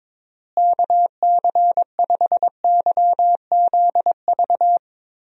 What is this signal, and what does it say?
Morse code, heard as a keyed tone, about 22 wpm, 700 Hz. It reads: KC5YZV
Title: USA Call Signs - 6 Characters - Repeated 3x - 22wpm